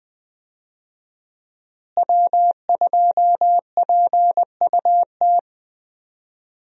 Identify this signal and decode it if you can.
Morse code, heard as a keyed tone, about 20 wpm, 700 Hz. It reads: W2PUT
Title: USA Call Signs - 5 Characters - Repeated 2x - 20wpm